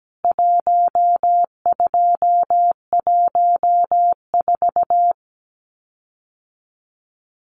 Morse code, heard as a keyed tone, about 17 wpm, 700 Hz. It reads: 1214
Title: CWT Exchange 17wpm